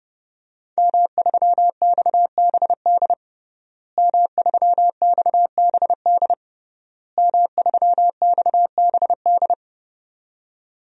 Morse code, heard as a keyed tone, about 30 wpm, 700 Hz. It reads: M3XBD M3XBD M3XBD
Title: DXCC-100 Call Signs - 5 Characters - Repeated 3x - 30wpm